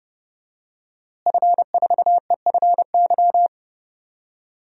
Morse code, heard as a keyed tone, about 30 wpm, 700 Hz. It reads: F4EFY
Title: DXCC-100 Call Signs - 5 Characters - Repeated 3x - 30wpm